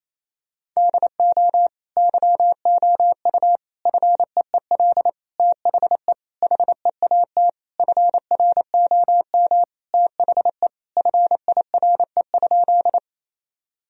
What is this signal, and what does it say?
Morse code, heard as a keyed tone, about 28 wpm, 700 Hz. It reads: DO YOU FEEL THE HEAT FROM THE FIRE?
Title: Sentences from Top 1000 Words - Review Part 3 - 28wpm